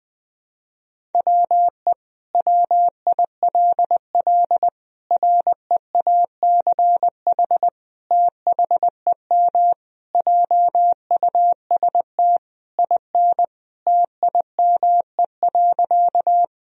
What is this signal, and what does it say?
Morse code, heard as a keyed tone, about 21 wpm, 700 Hz. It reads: WE WILL REACH THEM JUST IN TIME.
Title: Sentences from Top 1000 Words - Review Part 2 - 20wpm